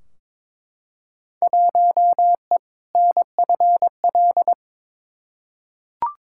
Morse code, heard as a keyed tone, about 22 wpm, 700 Hz. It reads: 1E NFL
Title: Field Day Exchanges - 22wpm